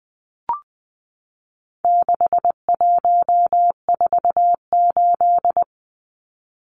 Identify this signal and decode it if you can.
Morse code, heard as a keyed tone, about 20 wpm, 700 Hz. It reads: 6148
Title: Numbers - 4-Digit 20wpm